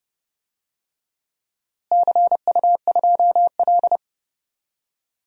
Morse code, heard as a keyed tone, about 30 wpm, 700 Hz. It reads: CU2L
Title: DXCC-100 Call Signs - 4 Characters - Repeated 3x - 30wpm